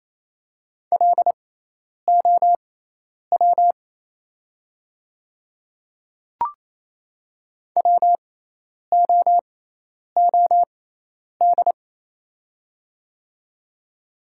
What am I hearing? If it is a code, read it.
Morse code, heard as a keyed tone, about 28 wpm, spaced as for 10 wpm, 690 Hz. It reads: LOW WOOD
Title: 6x Character Spacing - Top 300 Words - 28wpm